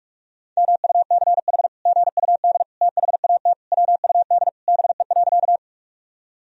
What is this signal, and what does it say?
Morse code, heard as a keyed tone, about 41 wpm, 700 Hz. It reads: MUCH GUD THAT WUD BE.
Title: Sentences from Top 200 Words and Abbreviations 45wpm